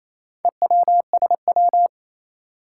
Morse code, heard as a keyed tone, about 27 wpm, 700 Hz. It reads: EWSW
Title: Four Challenging Letter-Number Combinations 28wpm